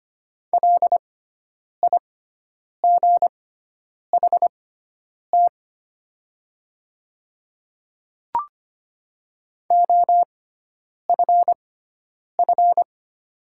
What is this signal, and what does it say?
Morse code, heard as a keyed tone, about 25 wpm, spaced as for 9 wpm, 700 Hz. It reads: LIGHT OFF
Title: 6x Character Spacing - Top 300 Words - 25wpm